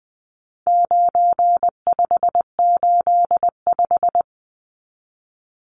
Morse code, heard as a keyed tone, about 20 wpm, 700 Hz. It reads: 9585
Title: Numbers - 4-Digit 20wpm